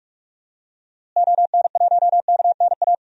Morse code, heard as a keyed tone, about 43 wpm, 700 Hz. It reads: ON1KNA
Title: DXCC-100 Call Signs - 6 Characters - Repeated 3x - 45wpm